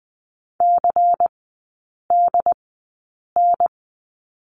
Morse code, heard as a keyed tone, about 19 wpm, spaced as for 8 wpm, 700 Hz. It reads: CDN
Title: 127 Lesson 21e - Number 6 - Three Characters 20wpm